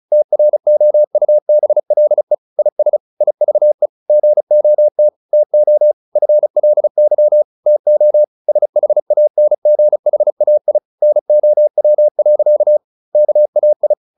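Morse code, heard as a keyed tone, about 34 wpm, 600 Hz. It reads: TROUBLE IS IVE GOT TO FLY TO SHANGHAI NOW. KAI
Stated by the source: Morse practice at 35 wpm from CW med Gnister, sending Ken Follett's novel Never in English.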